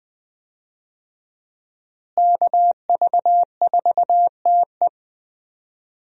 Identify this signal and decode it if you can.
Morse code, heard as a keyed tone, about 20 wpm, 700 Hz. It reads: KV4TE
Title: USA Call Signs - 5 Characters - Repeated 2x - 20wpm